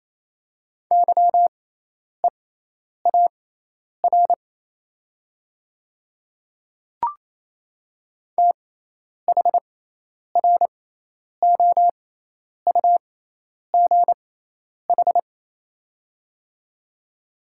Morse code, heard as a keyed tone, about 27 wpm, spaced as for 10 wpm, 690 Hz. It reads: YEAR THROUGH
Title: 6x Character Spacing - Top 300 Words - 28wpm